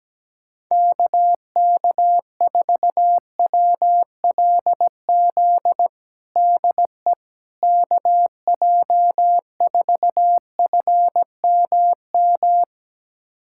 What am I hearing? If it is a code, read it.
Morse code, heard as a keyed tone, about 17 wpm, 700 Hz. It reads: KK4WLZ DE KJ4FMM
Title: USA Callsign DE Callsign - 6 Characters - 17wpm